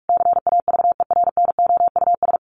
Morse code, heard as a keyed tone, about 44 wpm, 700 Hz. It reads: CAVERNOUS
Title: A Sound Of Thunder 45wpm